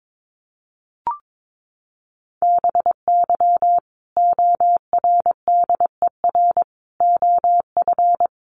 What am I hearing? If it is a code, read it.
Morse code, heard as a keyed tone, about 22 wpm, 700 Hz. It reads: BY ORDER OF
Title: Sentences from Top 1300 Words 22wpm